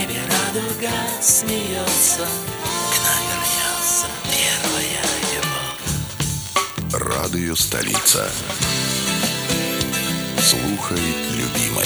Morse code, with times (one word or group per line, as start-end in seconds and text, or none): none